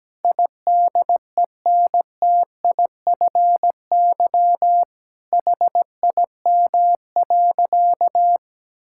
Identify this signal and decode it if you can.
Morse code, heard as a keyed tone, about 17 wpm, 700 Hz. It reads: IDENTIFY HIM.
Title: Sentences from Top 1000 Words - Review Part 6 - 17wpm